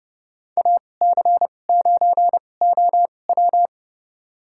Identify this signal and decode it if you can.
Morse code, heard as a keyed tone, about 31 wpm, spaced as for 22 wpm, 700 Hz. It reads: AC9OW
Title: USA Call Signs - 5 Characters - 2x Character Spacing - 30wpm